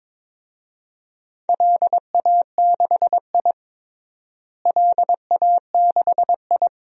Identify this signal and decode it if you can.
Morse code, heard as a keyed tone, about 22 wpm, 700 Hz. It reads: LA6I LA6I
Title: DXCC-100 Call Signs - 4 Characters - Repeated 3x - 22wpm